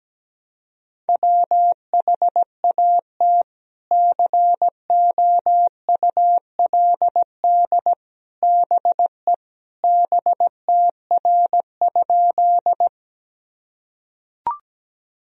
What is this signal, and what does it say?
Morse code, heard as a keyed tone, about 17 wpm, 700 Hz. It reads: WHAT COULD BE BTR?
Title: Sentences from Top 400 Words and Abbreviations 17wpm